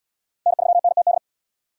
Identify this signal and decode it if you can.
Morse code, heard as a keyed tone, about 72 wpm, 700 Hz. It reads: N3NTD